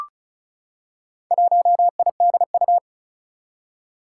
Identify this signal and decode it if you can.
Morse code, heard as a keyed tone, about 36 wpm, 700 Hz. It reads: 1IDU
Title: Four Challenging Letter-Number Combinations 35wpm